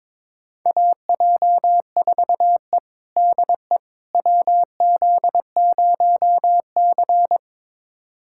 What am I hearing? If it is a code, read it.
Morse code, heard as a keyed tone, about 22 wpm, 700 Hz. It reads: AJ4E DE WZ0C
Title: USA Callsign DE Callsign - 4 Characters - 22wpm